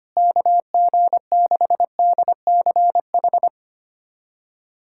Morse code, heard as a keyed tone, about 25 wpm, 700 Hz. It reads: KG6DCH
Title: USA Callsign DE Callsign 25wpm